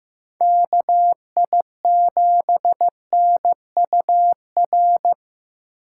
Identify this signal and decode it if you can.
Morse code, heard as a keyed tone, about 18 wpm, 700 Hz. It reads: KI7NUR
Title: Speed Racing - USA 2x3 Call Signs 15wpm